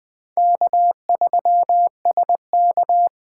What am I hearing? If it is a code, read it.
Morse code, heard as a keyed tone, about 21 wpm, 700 Hz. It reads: K3SK
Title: USA Call Signs - 4 Characters - 8x Character Spacing - 20wpm